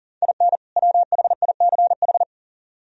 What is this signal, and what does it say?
Morse code, heard as a keyed tone, about 49 wpm, 700 Hz. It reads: IN WHICH
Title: Speed-Racing - Sets of 2 Words 40wpm